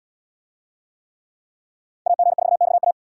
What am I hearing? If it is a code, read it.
Morse code, heard as a keyed tone, about 78 wpm, 700 Hz. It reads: AG4XA